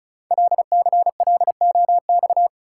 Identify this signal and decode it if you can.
Morse code, heard as a keyed tone, about 36 wpm, 700 Hz. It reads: LCLOX
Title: Code Groups - 5 Letter:Numbers - 35wpm